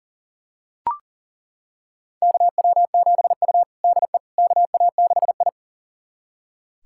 Morse code, heard as a keyed tone, about 40 wpm, 700 Hz. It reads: KW7U DE KA6I
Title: USA Callsign DE Callsign - 4 Characters - 40wpm